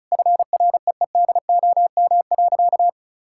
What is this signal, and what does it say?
Morse code, heard as a keyed tone, about 34 wpm, 700 Hz. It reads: FREEDOM.
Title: Sentences from Top 1000 Words - Review Part 5 - 35wpm